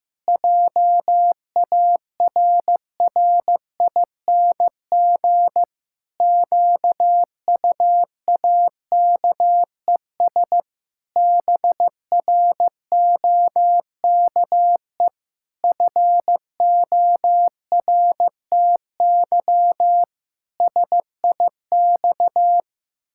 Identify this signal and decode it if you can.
Morse code, heard as a keyed tone, about 16 wpm, 700 Hz. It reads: JARRING QUAKES BROKE FORTY SIX
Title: Pangram Challenge v2 - 15wpm